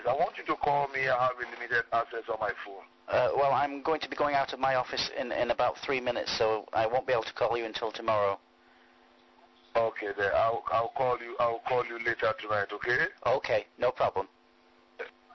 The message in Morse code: none